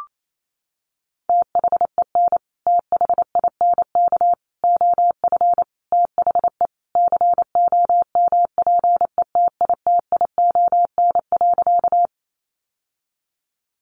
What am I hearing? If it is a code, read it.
Morse code, heard as a keyed tone, about 28 wpm, 700 Hz. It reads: THEN THINK OF THE COMPETITION.
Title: Sentences from Top 1000 Words 28wpm